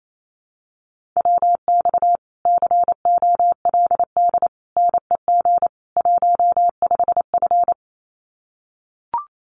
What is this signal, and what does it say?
Morse code, heard as a keyed tone, about 28 wpm, 700 Hz. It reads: WX COLD NEG 15F
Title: Common QSO Phrases Encore 28wpm